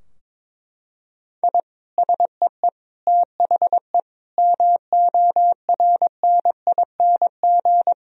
Morse code, heard as a keyed tone, about 22 wpm, 700 Hz. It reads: I SEE THE MORNING
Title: Sentences from Top 1000 Words - Review Part 2 - 22wpm